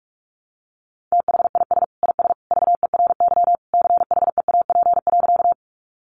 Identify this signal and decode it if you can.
Morse code, heard as a keyed tone, about 44 wpm, 700 Hz. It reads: THIS IS VERY CHEAP.
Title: Sentences from Top 1200 Words 45wpm